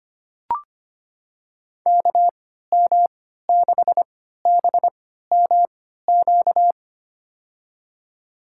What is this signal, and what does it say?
Morse code, heard as a keyed tone, about 25 wpm, spaced as for 14 wpm, 700 Hz. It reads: KM6BMQ
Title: USA Call Signs - 6 Characters - 3x Character Spacing - 25wpm